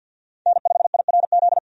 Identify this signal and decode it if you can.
Morse code, heard as a keyed tone, about 52 wpm, 700 Hz. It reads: N5IRZ
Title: USA Call Signs - 5 Characters - 50wpm